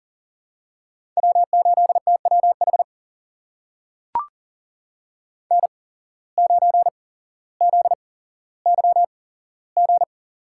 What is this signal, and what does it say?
Morse code, heard as a keyed tone, about 39 wpm, spaced as for 17 wpm, 700 Hz. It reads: W8TWH N9ZYG
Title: USA Call Signs - 5 Characters - 8x Character Spacing - 40wpm